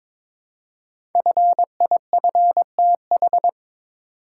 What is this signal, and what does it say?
Morse code, heard as a keyed tone, about 22 wpm, 700 Hz. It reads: FIFTH
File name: 5 Letter Words - First 3 Letters Given - 22wpm